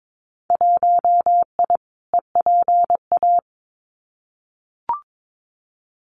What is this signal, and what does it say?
Morse code, heard as a keyed tone, about 22 wpm, 700 Hz. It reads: 1I EPA
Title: Winter Field Day Exchanges Encore 22wpm